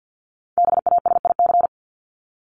Morse code, heard as a keyed tone, about 50 wpm, 700 Hz. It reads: BASIC
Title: Top 501-750 Words 50wpm